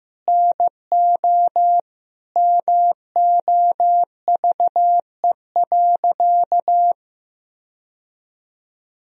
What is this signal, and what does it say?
Morse code, heard as a keyed tone, about 15 wpm, 700 Hz. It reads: NO MOVE.
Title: Sentences from Top 1000 Words - Review Part 1 - 15wpm